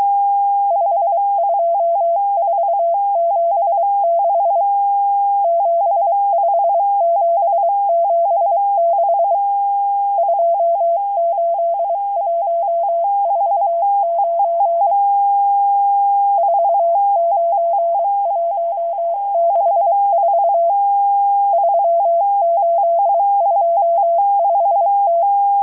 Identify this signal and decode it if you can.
Morse code, heard as a keyed tone, about 23 wpm, 680 Hz. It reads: 52476 75776 28149 49164 3825T
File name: HM02- -0437utc-7351khz